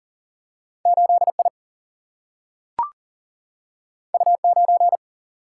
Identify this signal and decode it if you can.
Morse code, heard as a keyed tone, about 40 wpm, 700 Hz. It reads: 8 I U 9
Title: Single-Letter Single-Number 40wpm